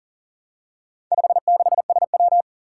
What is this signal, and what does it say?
Morse code, heard as a keyed tone, about 39 wpm, 700 Hz. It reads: 56SW
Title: Four Challenging Letter-Number Combinations 40wpm